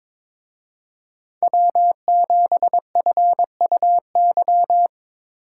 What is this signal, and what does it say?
Morse code, heard as a keyed tone, about 25 wpm, 700 Hz. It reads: W7FUY
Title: Speed Racing - USA 1x3 Call Signs 22wpm